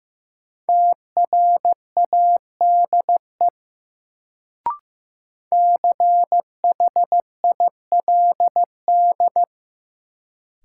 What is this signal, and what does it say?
Morse code, heard as a keyed tone, about 16 wpm, 700 Hz. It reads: TRADE CHILD
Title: Sets of 5 Words Encore - New Word Drill - 15wpm